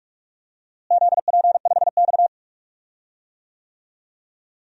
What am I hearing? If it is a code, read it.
Morse code, heard as a keyed tone, about 41 wpm, 700 Hz. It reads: ZP5X
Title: DXCC-100 Call Signs - 4 Characters - Repeated 2x - 45wpm